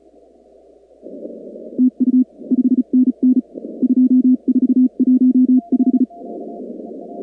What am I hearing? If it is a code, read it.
Morse code, heard as a keyed tone, about 36 wpm, 250 Hz. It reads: TU 5NN 2415
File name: contests / 2019-cw-wpx-kt5j-k5pi- -2330 0-45